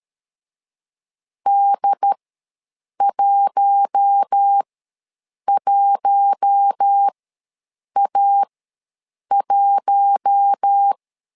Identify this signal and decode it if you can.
Morse code, heard as a keyed tone, about 13 wpm, spaced as for 7 wpm, 800 Hz. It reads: D11A1
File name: LEZIONI / 4 Q1 ws 12wpm 2129